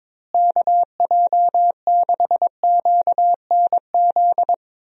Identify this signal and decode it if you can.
Morse code, heard as a keyed tone, about 22 wpm, 700 Hz. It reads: KJ6QNZ
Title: USA Call Signs - 6 Characters - Repeated 2x - 22wpm